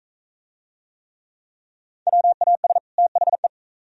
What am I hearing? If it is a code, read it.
Morse code, heard as a keyed tone, about 39 wpm, 700 Hz. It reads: WAS THE